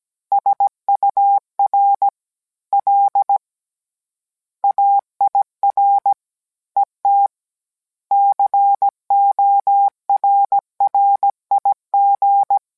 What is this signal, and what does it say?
Morse code, heard as a keyed tone, about 17 wpm, 800 Hz. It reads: SUR L AIR ET CORRIG